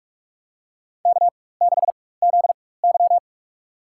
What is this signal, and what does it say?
Morse code, heard as a keyed tone, about 46 wpm, spaced as for 21 wpm, 700 Hz. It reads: K6ZY
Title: USA Call Signs - 4 Characters - 4x Character Spacing - 45wpm